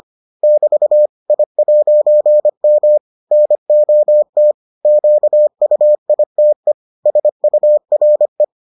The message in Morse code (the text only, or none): <BT> I'M NOT QUITE SURE